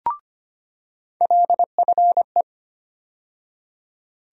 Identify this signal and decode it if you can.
Morse code, heard as a keyed tone, about 25 wpm, 700 Hz. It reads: LFE
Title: USA Call Sign Postfixes - 3 Letters 25wpm